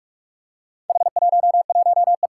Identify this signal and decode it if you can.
Morse code, heard as a keyed tone, about 45 wpm, 700 Hz. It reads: H11E
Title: Four Challenging Letter-Number Combinations 45wpm